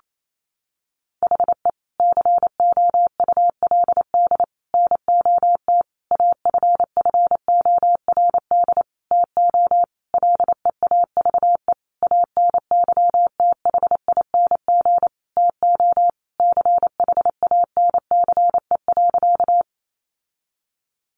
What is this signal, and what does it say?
Morse code, heard as a keyed tone, about 28 wpm, 700 Hz. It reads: HE COULD NOT AFFORD TO LEAVE ANYTHING TO CHANCE.
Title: Sentences from Top 1100 Words 28wpm